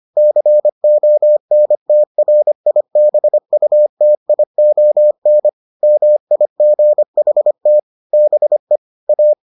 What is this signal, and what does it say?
Morse code, heard as a keyed tone, about 25 wpm, 600 Hz. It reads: CONTRIBUTION MIGHT BE A